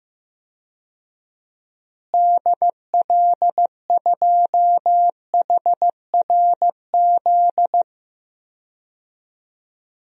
Morse code, heard as a keyed tone, about 15 wpm, 700 Hz. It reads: DL2HRZ
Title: DXCC-100 Call Signs - 6 Characters - Repeated 3x - 15wpm